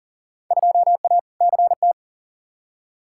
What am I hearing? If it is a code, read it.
Morse code, heard as a keyed tone, about 40 wpm, 700 Hz. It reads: 2A CT
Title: Field Day Exchanges - 40wpm